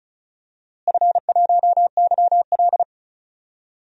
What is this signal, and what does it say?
Morse code, heard as a keyed tone, about 34 wpm, 700 Hz. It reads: F1YL
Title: DXCC-100 Call Signs - 4 Characters - Repeated 3x - 35wpm